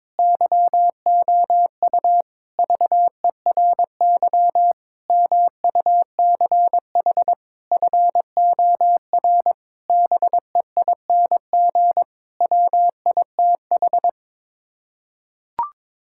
Morse code, heard as a keyed tone, about 22 wpm, 700 Hz. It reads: YOU VERY MUCH FOR BEING WITH
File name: Sets of 9 Words - 22wpm